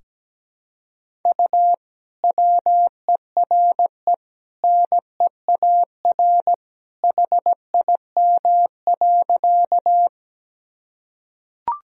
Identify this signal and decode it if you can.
Morse code, heard as a keyed tone, about 17 wpm, 700 Hz. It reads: U WERE NEAR HIM.
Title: Sentences from Top 200 Words and Abbreviations 17wpm